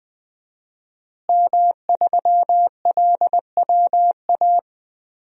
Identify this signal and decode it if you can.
Morse code, heard as a keyed tone, about 20 wpm, 700 Hz. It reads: M3LWA
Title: DXCC-100 Call Signs - 5 Characters - Repeated 3x - 20wpm